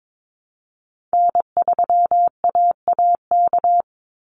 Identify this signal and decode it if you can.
Morse code, heard as a keyed tone, about 22 wpm, 700 Hz. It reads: N3AAK
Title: Call Signs Repeated 3x - 22wpm